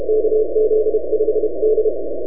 Morse code, heard as a keyed tone, about 33 wpm, 470 Hz. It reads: KZ5D
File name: NAQPCW-8-2009 / W1UJ NAQP CW 09 1800Z